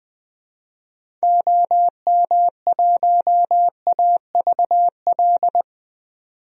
Morse code, heard as a keyed tone, about 20 wpm, 700 Hz. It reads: OM1AVL
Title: DXCC-100 Call Signs - 6 Characters - 20wpm